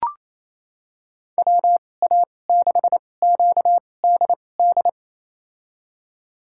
Morse code, heard as a keyed tone, about 28 wpm, spaced as for 20 wpm, 700 Hz. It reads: WA6QDD
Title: USA Call Signs - 6 Characters - 2x Character Spacing - 28wpm